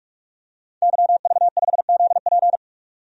Morse code, heard as a keyed tone, about 43 wpm, 700 Hz. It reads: YV5ZP
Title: DXCC-100 Call Signs - 5 Characters - 45wpm